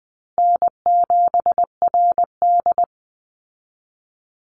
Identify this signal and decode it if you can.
Morse code, heard as a keyed tone, about 20 wpm, 700 Hz. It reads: N7RD
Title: Easy Call Signs Repeated 3x - 20wpm